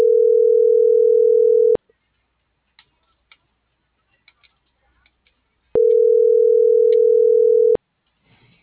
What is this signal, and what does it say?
Ringback tone (North America), 440 + 480 Hz.